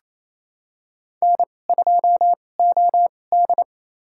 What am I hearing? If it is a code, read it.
Morse code, heard as a keyed tone, about 28 wpm, spaced as for 20 wpm, 700 Hz. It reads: N2OD
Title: USA Call Signs - 4 Characters - 2x Character Spacing - 28wpm